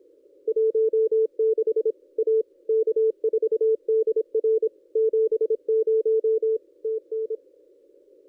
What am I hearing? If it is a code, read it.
Morse code, heard as a keyed tone, about 26 wpm, 440 Hz. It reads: 16 A K4DR 70 TN
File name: contests / ss-cw-k5tr-op-n5rz -1400